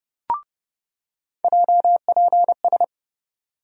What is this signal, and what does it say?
Morse code, heard as a keyed tone, about 30 wpm, 700 Hz. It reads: JPS
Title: USA Call Sign Postfixes - 3 Letters 30wpm